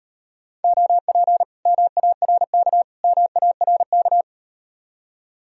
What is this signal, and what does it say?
Morse code, heard as a keyed tone, about 38 wpm, 700 Hz. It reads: OP MARK MARK